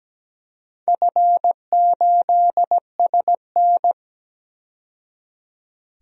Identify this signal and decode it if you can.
Morse code, heard as a keyed tone, about 17 wpm, 700 Hz. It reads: F8SN
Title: DXCC-100 Call Signs - 4 Characters - 17wpm